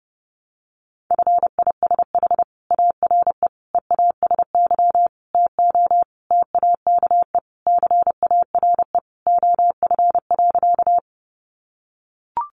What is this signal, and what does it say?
Morse code, heard as a keyed tone, about 30 wpm, 700 Hz. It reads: FISH ARE EASY TO TAKE CARE OF.
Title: Additional Easy Phrases 30wpm